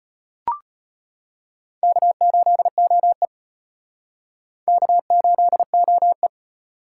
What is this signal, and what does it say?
Morse code, heard as a keyed tone, about 33 wpm, 700 Hz. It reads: K8OE K8OE